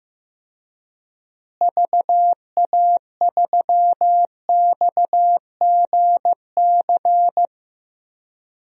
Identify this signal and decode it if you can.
Morse code, heard as a keyed tone, about 15 wpm, 700 Hz. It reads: VA3XGC
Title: DXCC-100 Call Signs - 6 Characters - Repeated 3x - 15wpm